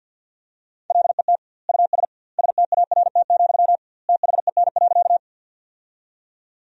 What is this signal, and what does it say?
Morse code, heard as a keyed tone, about 52 wpm, 700 Hz. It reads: LET US START, THEN.
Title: 2x Word-Spacing - Sentences from Top 300 Words - 50wpm